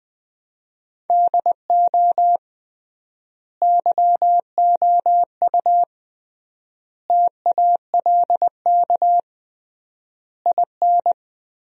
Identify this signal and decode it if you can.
Morse code, heard as a keyed tone, about 20 wpm, 700 Hz. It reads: DO YOU TALK IN
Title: 3x Word-Spacing - Sentences from Top 500 Words - 20wpm